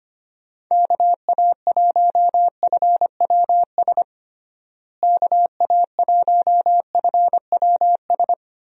Morse code, heard as a keyed tone, about 25 wpm, 700 Hz. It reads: KA1FWS KA1FWS
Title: USA Call Signs - 6 Characters - Repeated 2x - 25wpm